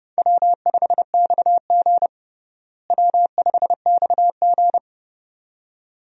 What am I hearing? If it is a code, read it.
Morse code, heard as a keyed tone, about 32 wpm, 700 Hz. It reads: W5XG W5XG
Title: USA Call Signs - 4 Characters - Repeated 2x - 30wpm